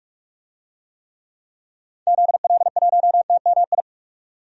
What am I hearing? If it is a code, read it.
Morse code, heard as a keyed tone, about 42 wpm, 700 Hz. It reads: ZL1TMI
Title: DXCC-100 Call Signs - 6 Characters - 45wpm